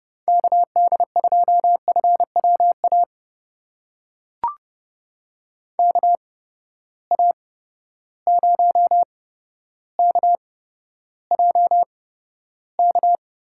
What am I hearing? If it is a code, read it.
Morse code, heard as a keyed tone, about 31 wpm, spaced as for 13 wpm, 700 Hz. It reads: KD2FWA KA0KJK
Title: USA Call Signs - 6 Characters - 8x Character Spacing - 30wpm